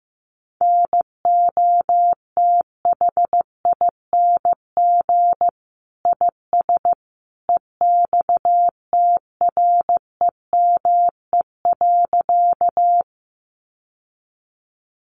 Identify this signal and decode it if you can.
Morse code, heard as a keyed tone, about 14 wpm, 700 Hz. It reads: NOTHING IS EXTREME.
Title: Sentences from Top 1300 Words 15wpm